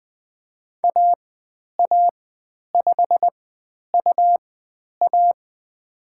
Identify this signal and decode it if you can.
Morse code, heard as a keyed tone, about 20 wpm, spaced as for 10 wpm, 700 Hz. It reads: AA5UA
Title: 109 Lesson 18g - Letter m - Call Signs 20wpm